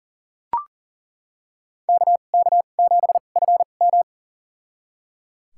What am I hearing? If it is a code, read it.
Morse code, heard as a keyed tone, about 38 wpm, spaced as for 28 wpm, 700 Hz. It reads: KK7FM
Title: USA Call Signs - 5 Characters - 2x Character Spacing - 40wpm